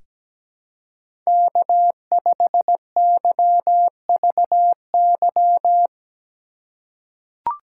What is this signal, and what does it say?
Morse code, heard as a keyed tone, about 17 wpm, 700 Hz. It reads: K5YVY
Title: K1USN SST QSO Practice - 17wpm